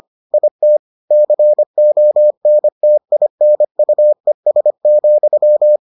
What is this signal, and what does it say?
Morse code, heard as a keyed tone, about 25 wpm, 600 Hz. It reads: IT CONTINUES,